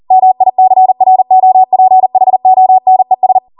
Morse code, heard as a keyed tone, about 40 wpm, 750 Hz. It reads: MICROPHONES